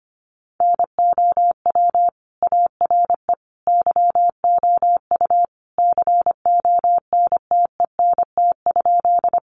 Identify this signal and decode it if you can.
Morse code, heard as a keyed tone, about 25 wpm, 700 Hz. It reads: NOW ARE YOU CONTENT?
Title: Sentences from Top 900 Words 25wpm